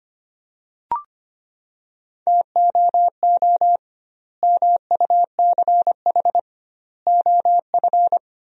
Morse code, heard as a keyed tone, about 25 wpm, 700 Hz. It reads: TOO MUCH OF
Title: 2x Word-Spacing - Sentences from Top 200 Words - 25wpm